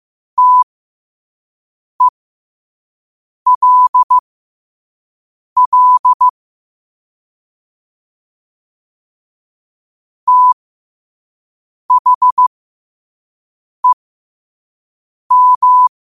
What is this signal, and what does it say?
Morse code, heard as a keyed tone, about 15 wpm, spaced as for 5 wpm, 1000 Hz. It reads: TELL THEM
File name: qotdfiles / 5wpm